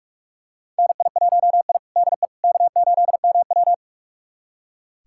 Morse code, heard as a keyed tone, about 46 wpm, 700 Hz. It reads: NI1I DE K8MW
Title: USA Callsign DE Callsign - 4 Characters - 45wpm